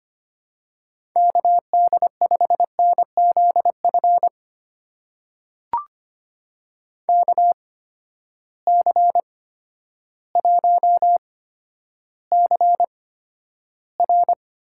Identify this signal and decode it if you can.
Morse code, heard as a keyed tone, about 25 wpm, spaced as for 11 wpm, 700 Hz. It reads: KD5NZF KC1CR
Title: USA Call Signs - 6 Characters - 8x Character Spacing - 25wpm